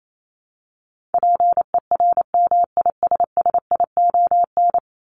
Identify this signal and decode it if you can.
Morse code, heard as a keyed tone, about 28 wpm, 700 Hz. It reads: PERMISSION
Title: Top 1501-1750 Words 28wpm